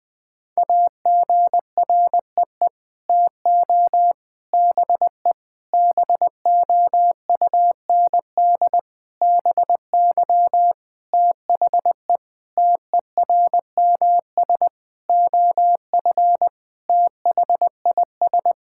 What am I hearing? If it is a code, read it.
Morse code, heard as a keyed tone, about 20 wpm, 700 Hz. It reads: AGREE TO BE BOUND BY THE TERMS OF THIS
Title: Sets of 9 Words - 20wpm